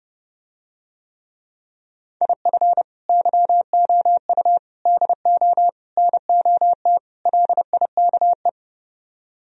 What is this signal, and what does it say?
Morse code, heard as a keyed tone, about 30 wpm, 700 Hz. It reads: IF YOU DO NOT LIKE